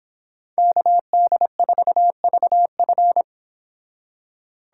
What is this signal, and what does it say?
Morse code, heard as a keyed tone, about 31 wpm, 700 Hz. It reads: KD4VF